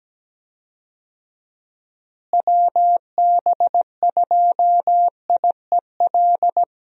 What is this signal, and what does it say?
Morse code, heard as a keyed tone, about 17 wpm, 700 Hz. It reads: WB2IEL
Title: USA Call Signs - 6 Characters - Repeated 2x - 17wpm